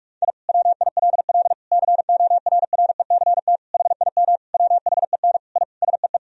Morse code, heard as a keyed tone, about 41 wpm, 700 Hz. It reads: I WILL CORRECT HIM WHEN I SEE
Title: Sentences from Top 1000 Words - Review Part 2 - 45wpm